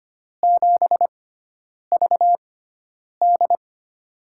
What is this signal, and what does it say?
Morse code, heard as a keyed tone, about 25 wpm, spaced as for 9 wpm, 700 Hz. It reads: 7VD
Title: USA Call Signs - 4 Characters - 6x Character Spacing - 25wpm